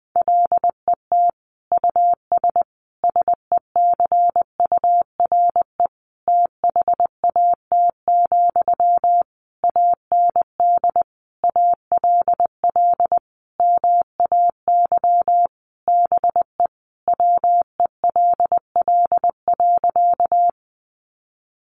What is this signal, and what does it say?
Morse code, heard as a keyed tone, about 20 wpm, 700 Hz. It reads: LET US SECURE THAT, AND ALL MAY BE WELL.
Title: Sentences from Top 1200 Words 20wpm